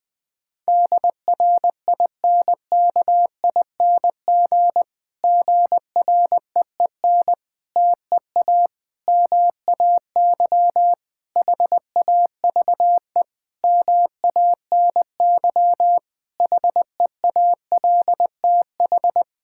Morse code, heard as a keyed tone, about 21 wpm, 700 Hz. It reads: DRINKING GREEN TEA MAY HAVE MANY HEALTH
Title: Difficult Sentences with Hints - 20wpm